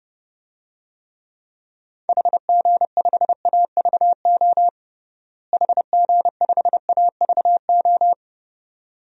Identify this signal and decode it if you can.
Morse code, heard as a keyed tone, about 30 wpm, 700 Hz. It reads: HG5AVO HG5AVO
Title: DXCC-100 Call Signs - 6 Characters - Repeated 2x - 30wpm